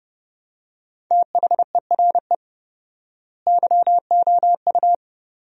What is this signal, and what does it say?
Morse code, heard as a keyed tone, about 30 wpm, 700 Hz. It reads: THERE YOU
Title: 4x Word-Spacing - Sentences from Top 100 Words - 30wpm